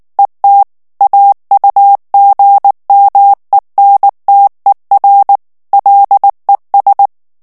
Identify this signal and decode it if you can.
Morse code, heard as a keyed tone, about 19 wpm, 800 Hz. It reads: ET AUGMENTER LES
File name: morse mp3 / FRANCAIS TEXTE CLAIR 1 VITESSE 18 MOTS MINUTES